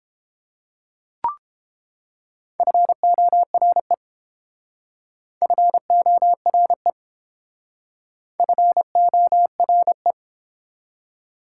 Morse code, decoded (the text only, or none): FORE FORE FORE